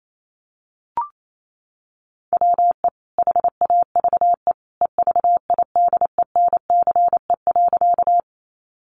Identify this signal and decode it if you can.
Morse code, heard as a keyed tone, about 28 wpm, 700 Hz. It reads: WE HAVE EVIDENCE.
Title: Sentences from Top 1300 Words 28wpm